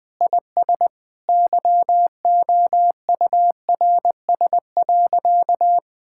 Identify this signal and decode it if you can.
Morse code, heard as a keyed tone, about 20 wpm, 700 Hz. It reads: IS YOURS.
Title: Sentences with 8-Word Phrases - Spoken Beginning - 20wpm